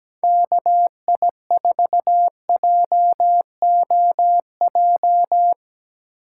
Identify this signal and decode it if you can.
Morse code, heard as a keyed tone, about 17 wpm, 700 Hz. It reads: KI4JOJ
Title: USA Call Signs - 6 Characters - 4x Character Spacing - 17wpm